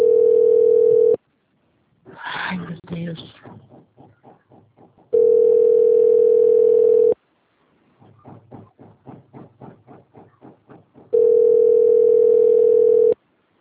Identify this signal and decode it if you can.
Ringback tone (North America), 440 + 480 Hz, with speech.
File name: rg-1600-unknown- - - .42613